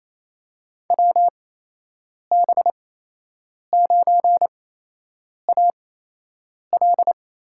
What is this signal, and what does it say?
Morse code, heard as a keyed tone, about 28 wpm, spaced as for 8 wpm, 700 Hz. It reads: WB9AL
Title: USA Call Signs - 5 Characters - 8x Character Spacing - 28wpm